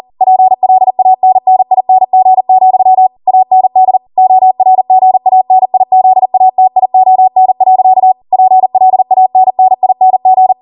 Morse code, heard as a keyed tone, about 40 wpm, 750 Hz. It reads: PLANNING, AND ORGANIZATION. PLANNING